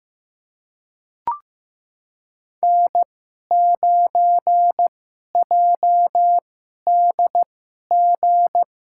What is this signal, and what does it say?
Morse code, heard as a keyed tone, about 16 wpm, spaced as for 11 wpm, 700 Hz. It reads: N9JDG
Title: USA Call Signs - 5 Characters - 2x Character Spacing - 15wpm